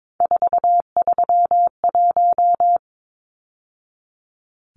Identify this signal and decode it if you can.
Morse code, heard as a keyed tone, about 22 wpm, 700 Hz. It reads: 431
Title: Numbers - 3-Digit 22wpm